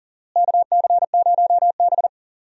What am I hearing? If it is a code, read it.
Morse code, heard as a keyed tone, about 40 wpm, 700 Hz. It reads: KC0B
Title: USA Call Signs - 4 Characters - Repeated 2x - 40wpm